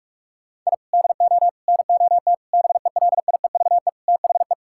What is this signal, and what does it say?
Morse code, heard as a keyed tone, about 46 wpm, 690 Hz. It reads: I DO NOT BELIEVE THE
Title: Sentences from Top 1000 Words - Review Part 6 - 45wpm